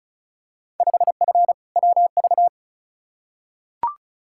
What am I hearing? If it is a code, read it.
Morse code, heard as a keyed tone, about 34 wpm, 700 Hz. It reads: 5F WV
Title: Field Day Exchanges Encore - 35wpm